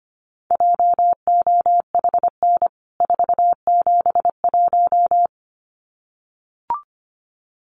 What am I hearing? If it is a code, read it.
Morse code, heard as a keyed tone, about 25 wpm, 700 Hz. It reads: JOHN 471
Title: CWT Exchange 25wpm